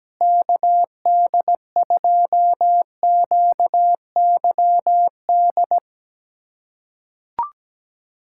Morse code, heard as a keyed tone, about 17 wpm, 700 Hz. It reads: KD2QYD
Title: USA Callsign DE Callsign - 6 Characters - 17wpm